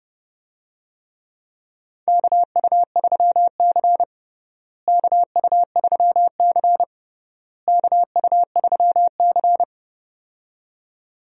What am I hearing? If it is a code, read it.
Morse code, heard as a keyed tone, about 32 wpm, 700 Hz. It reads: KU3C KU3C KU3C
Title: USA Call Signs - 4 Characters - Repeated 3x - 30wpm